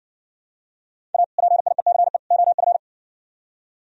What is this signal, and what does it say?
Morse code, heard as a keyed tone, about 60 wpm, 700 Hz. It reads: A PIECE OF